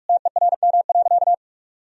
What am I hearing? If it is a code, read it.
Morse code, heard as a keyed tone, about 46 wpm, 700 Hz. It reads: TERM.
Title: Sentences from Top 1000 Words - Review Part 4 - 45wpm